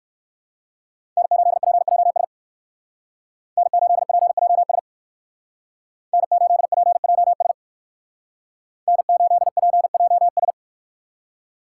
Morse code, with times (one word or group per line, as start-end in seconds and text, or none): 1.2-2.2 N8PJS
3.6-4.8 N8PJS
6.1-7.5 N8PJS
8.9-10.5 N8PJS